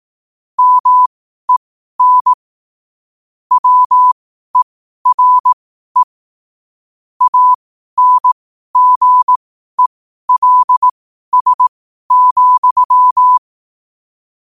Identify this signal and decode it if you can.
Morse code, heard as a keyed tone, about 18 wpm, spaced as for 12 wpm, 1000 Hz. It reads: MEN WERE ANGELS,
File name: qotdfiles / 12wpm